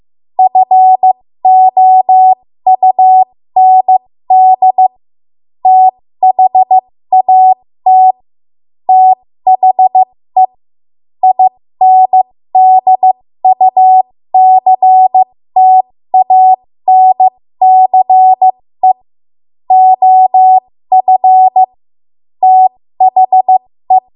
Morse code, heard as a keyed tone, about 15 wpm, spaced as for 13 wpm, 750 Hz. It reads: FOUND THAT THE INDUCTANCE OF THE
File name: Morse / 13WPM